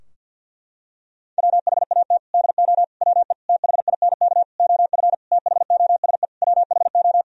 Morse code, heard as a keyed tone, about 51 wpm, 700 Hz. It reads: WHAT DO WE THINK OF THOSE WHO
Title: Sentences from Top 1000 Words - Review Part 2 - 50wpm